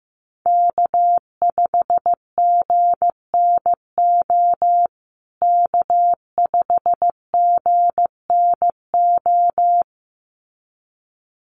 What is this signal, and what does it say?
Morse code, heard as a keyed tone, about 15 wpm, 700 Hz. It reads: K5GNO K5GNO
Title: USA Call Signs Repeated 2x 15wpm